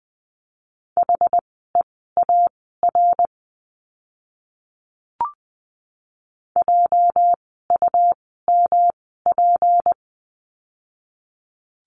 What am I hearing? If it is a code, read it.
Morse code, heard as a keyed tone, about 19 wpm, spaced as for 14 wpm, 700 Hz. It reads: HEAR JUMP
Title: 206 Lesson 33f - Letter j - Words 20wpm